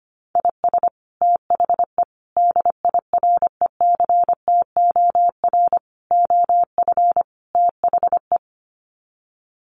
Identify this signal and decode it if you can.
Morse code, heard as a keyed tone, about 25 wpm, 700 Hz. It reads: IS THE DIRECTOR OF THE
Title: Sets of 5 Words Encore2 25wpm